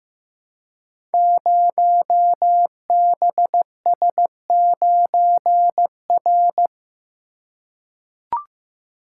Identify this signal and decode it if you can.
Morse code, heard as a keyed tone, about 15 wpm, 700 Hz. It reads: 0BS9R
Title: Code Groups - 5 Letter:Numbers - 15wpm